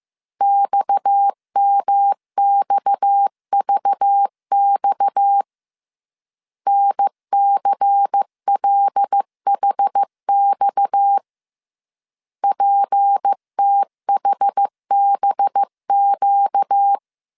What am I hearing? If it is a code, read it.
Morse code, heard as a keyed tone, about 15 wpm, 800 Hz. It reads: XMXVX NCLHX PTHBQ
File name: CW / DET / 1 6 1500C S C 14wpm 2951